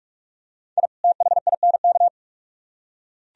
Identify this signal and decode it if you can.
Morse code, heard as a keyed tone, about 44 wpm, 700 Hz. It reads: I THINK